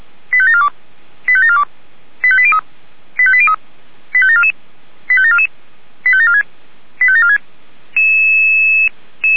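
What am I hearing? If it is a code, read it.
Five-tone selective call (ZVEI), heard as quick runs of five tones, tones of about 70 ms. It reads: 85742 85742 85803 85803 85744 85744 85746 85746
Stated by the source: ZVEI five-tone selective call received via SDR.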